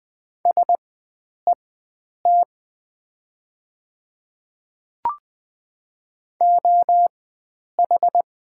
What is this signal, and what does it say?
Morse code, heard as a keyed tone, about 20 wpm, spaced as for 9 wpm, 700 Hz. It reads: SET OH
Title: 4x Character Spacing - Top 500 Words - 20wpm